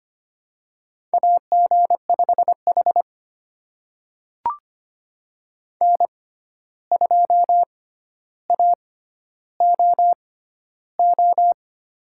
Morse code, heard as a keyed tone, about 25 wpm, spaced as for 12 wpm, 700 Hz. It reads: AG5H N2AOO
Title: USA Call Signs - 6x Character Spacing - 25wpm